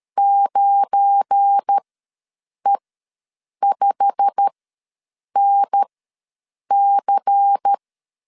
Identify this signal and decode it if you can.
Morse code, heard as a keyed tone, about 13 wpm, spaced as for 7 wpm, 800 Hz. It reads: 9E5NC
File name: LEZIONI / 8 YJ45 / 8 YJ45 ws 12wpm 3146